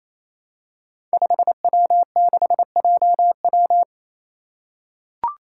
Morse code, heard as a keyed tone, about 28 wpm, 700 Hz. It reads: 5W6JW
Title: Five Challenging Letter-Number Combinations 28wpm